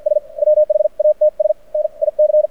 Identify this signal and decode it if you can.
Morse code, heard as a keyed tone, about 52 wpm, spaced as for 42 wpm, 610 Hz. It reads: S WHATS NIC
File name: archived / CRSnet